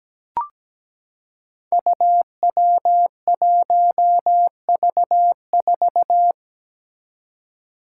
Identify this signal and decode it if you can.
Morse code, heard as a keyed tone, about 17 wpm, 700 Hz. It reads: UW1V4
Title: Five Challenging Letter-Number Combinations 17wpm